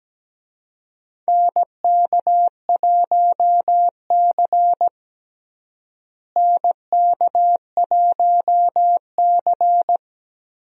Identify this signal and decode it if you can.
Morse code, heard as a keyed tone, about 17 wpm, 700 Hz. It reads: NK1C NK1C
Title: USA Call Signs - 4 Characters - Repeated 2x - 17wpm